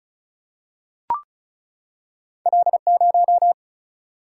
Morse code, heard as a keyed tone, about 36 wpm, 700 Hz. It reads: L0
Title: Single-Letter Single-Number 35wpm